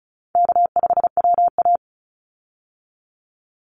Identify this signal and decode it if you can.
Morse code, heard as a keyed tone, about 34 wpm, 700 Hz. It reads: K5WA
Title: Easy Call Signs Repeated 3x - 35wpm